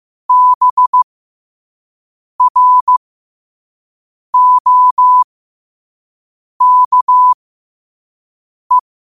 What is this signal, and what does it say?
Morse code, heard as a keyed tone, about 15 wpm, spaced as for 5 wpm, 1000 Hz. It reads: BROKE
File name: qotdfiles / 5wpm